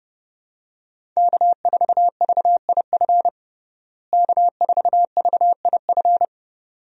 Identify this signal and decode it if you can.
Morse code, heard as a keyed tone, about 32 wpm, 700 Hz. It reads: K4VIF K4VIF
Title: USA Call Signs - 5 Characters - Repeated 2x - 30wpm